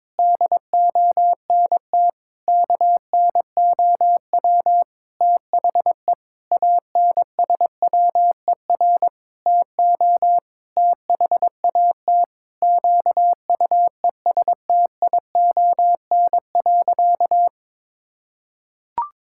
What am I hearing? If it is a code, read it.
Morse code, heard as a keyed tone, about 22 wpm, 700 Hz. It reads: DONT KNOW THE ANSWER TO THAT QUESTION.
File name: Sentences with 7-Word Phrases - 22wpm